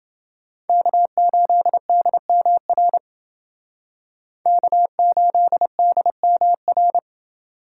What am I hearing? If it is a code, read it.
Morse code, heard as a keyed tone, about 29 wpm, 700 Hz. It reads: K8DMR K8DMR